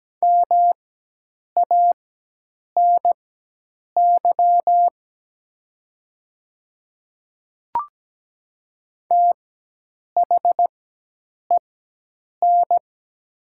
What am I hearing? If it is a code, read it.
Morse code, heard as a keyed tone, about 17 wpm, spaced as for 8 wpm, 700 Hz. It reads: MANY THEN
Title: 4x Character Spacing - Top 300 Words - 17wpm